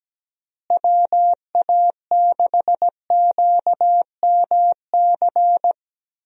Morse code, heard as a keyed tone, about 17 wpm, 700 Hz. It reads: WA6QMC
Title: USA Call Signs - 6 Characters - Repeated 2x - 17wpm